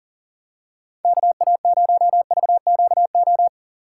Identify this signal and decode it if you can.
Morse code, heard as a keyed tone, about 40 wpm, 700 Hz. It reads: KA0VQO
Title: USA Call Signs - 6 Characters - 2x Character Spacing - 40wpm